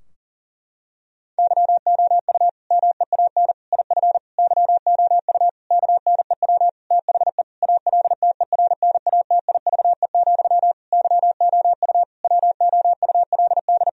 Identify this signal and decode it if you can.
Morse code, heard as a keyed tone, about 40 wpm, 700 Hz. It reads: YOU MEAN IF YOU KNEW THE ALTERNATIVE, YOU WOULD
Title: Sentences from Top 1000 Words - Review Part 4 - 40wpm